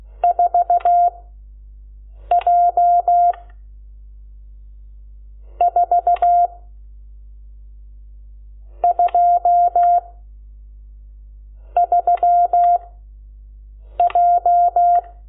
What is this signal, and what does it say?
Morse code, heard as a keyed tone, about 16 wpm, 700 Hz. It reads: 4 J 4 2 3 J
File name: Lecciones / CLASE Nº12